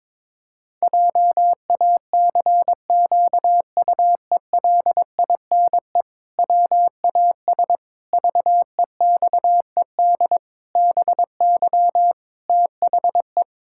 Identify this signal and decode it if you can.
Morse code, heard as a keyed tone, about 22 wpm, 700 Hz. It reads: JACQUELINE WAS VEXED BY THE
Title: Pangram Challenge v2 - 22wpm